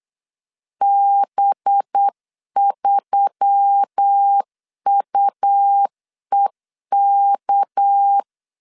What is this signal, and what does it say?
Morse code, heard as a keyed tone, about 9 wpm, 800 Hz. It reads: B3UEK
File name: CW / DET / 1 6 1500C S C 8wpm 5219